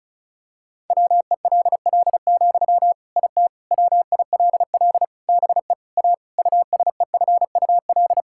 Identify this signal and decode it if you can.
Morse code, heard as a keyed tone, about 34 wpm, 700 Hz. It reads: WELL, IT WILL BE A USEFUL
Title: Sentences from Top 1000 Words - Review Part 6 - 35wpm